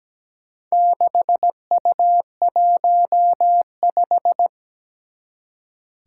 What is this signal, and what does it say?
Morse code, heard as a keyed tone, about 17 wpm, 700 Hz. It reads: 6U15
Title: Four Challenging Letter-Number Combinations 17wpm